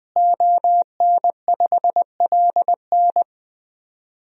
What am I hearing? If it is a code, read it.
Morse code, heard as a keyed tone, about 20 wpm, 700 Hz. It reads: ON5LN
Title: DXCC-100 Call Signs - 5 Characters - Repeated 2x - 20wpm